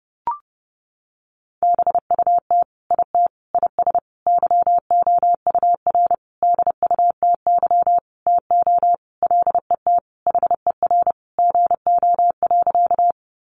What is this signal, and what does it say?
Morse code, heard as a keyed tone, about 30 wpm, 700 Hz. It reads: BUT IT IS YOUR DUTY TO LET HER GO.
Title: Sentences from Top 1500 Words 30wpm